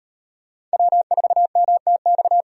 Morse code, heard as a keyed tone, about 38 wpm, 700 Hz. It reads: W4MTX